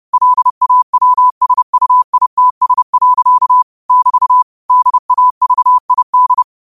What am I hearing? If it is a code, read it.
Morse code, heard as a keyed tone, about 30 wpm, 1000 Hz. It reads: LAWSUITS. <BT> DAVID
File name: qotdfiles / 30wpm